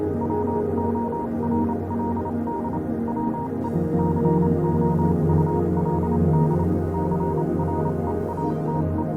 Morse code, heard as a keyed tone, about 28 wpm, 940 Hz, with noise. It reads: WORLD WE COULD CREATE
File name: Vylet Pony / aura / do you remember the song she sang (morse code version)